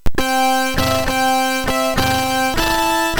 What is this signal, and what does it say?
There is no code.